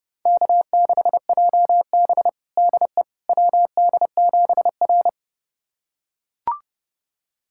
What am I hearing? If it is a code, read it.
Morse code, heard as a keyed tone, about 31 wpm, 700 Hz. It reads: K6JB DE WD7R
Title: USA Callsign DE Callsign - 4 Characters - 30wpm